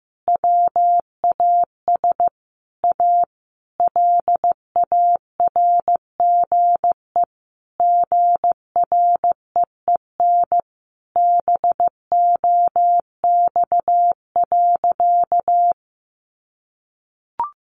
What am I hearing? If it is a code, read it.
Morse code, heard as a keyed tone, about 15 wpm, 700 Hz. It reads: WAS A LARGE GREEN BOX.
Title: Sentences from Top 500 Words 15wpm